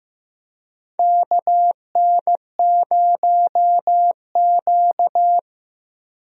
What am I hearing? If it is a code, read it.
Morse code, heard as a keyed tone, about 15 wpm, 700 Hz. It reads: KN0Q
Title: USA Call Signs - 4 Characters - Repeated 3x - 15wpm